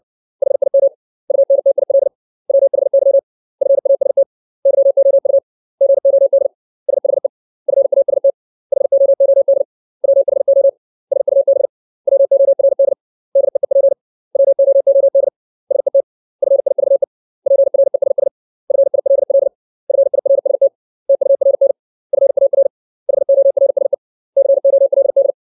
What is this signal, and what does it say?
Morse code, heard as a keyed tone, about 60 wpm, 550 Hz. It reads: HIM UNTIL WHY FAST YOU GOD SHE FAST HOOD WHO SUB WORD DEEP WOOD SET FIVE PRESS FILL FIRST TURN FAR HORSE COLD
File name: 632 words / 632 Words 60WPM